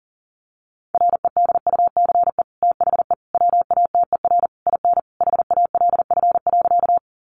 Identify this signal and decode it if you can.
Morse code, heard as a keyed tone, about 40 wpm, 700 Hz. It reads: REDUCE THE WATER IN HALF.
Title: Sentences from Top 1100 Words 40wpm